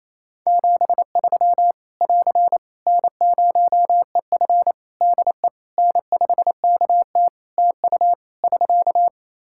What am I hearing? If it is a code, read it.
Morse code, heard as a keyed tone, about 28 wpm, 700 Hz. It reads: 73 <AR> N0EF DE N5KT TU <SK>
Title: Practice QSO - Exchange 4 - Ending - 28wpm